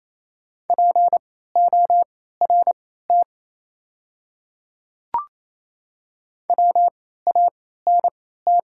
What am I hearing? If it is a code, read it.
Morse code, heard as a keyed tone, about 28 wpm, spaced as for 16 wpm, 700 Hz. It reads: PORT WANT
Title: 3x Character Spacing - Top 300 Words - 28wpm